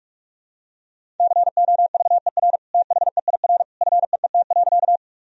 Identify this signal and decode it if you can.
Morse code, heard as a keyed tone, about 46 wpm, 700 Hz. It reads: COVER THEIR FEET.
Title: Sentences from Top 1000 Words - Review Part 1 - 45wpm